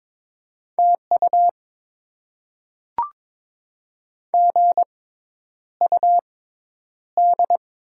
Morse code, heard as a keyed tone, about 22 wpm, spaced as for 10 wpm, 700 Hz. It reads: TU GUD
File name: Top 20 QSO Elements - 6x Character Spacing - 22wpm